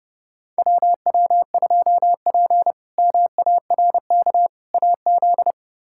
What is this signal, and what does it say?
Morse code, heard as a keyed tone, about 30 wpm, 700 Hz. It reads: WW2P MARK AZ
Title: K1USN SST QSO Practice - 30wpm